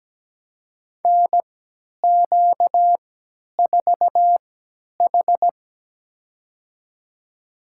Morse code, heard as a keyed tone, about 17 wpm, spaced as for 10 wpm, 700 Hz. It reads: NQ4H
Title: USA Call Signs - 3x Character Spacing - 17wpm